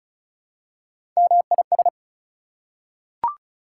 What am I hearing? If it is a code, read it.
Morse code, heard as a keyed tone, about 34 wpm, 700 Hz. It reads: MIS
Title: Prefixes and Suffixes 35wpm